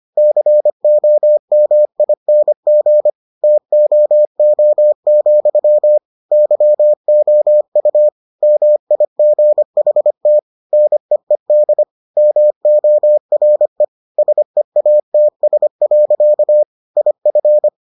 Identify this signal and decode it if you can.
Morse code, heard as a keyed tone, about 25 wpm, 600 Hz. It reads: COMING TOO, YOU MIGHT NEED MORE SEATS. IF